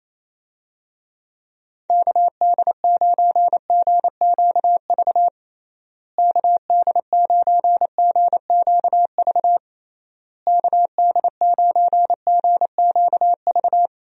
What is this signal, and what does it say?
Morse code, heard as a keyed tone, about 28 wpm, 700 Hz. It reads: KD9GQV KD9GQV KD9GQV
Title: USA Call Signs - 6 Characters - Repeated 3x - 28wpm